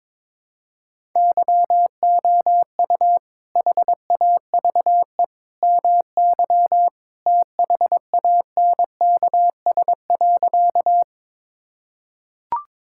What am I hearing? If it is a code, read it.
Morse code, heard as a keyed tone, about 22 wpm, 700 Hz. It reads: YOU HAVE MY THANKS.
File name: practice / 22wpm / Sentences with 2-Word Phrases - 22wpm